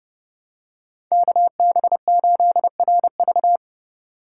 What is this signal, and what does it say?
Morse code, heard as a keyed tone, about 30 wpm, 700 Hz. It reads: KB8RV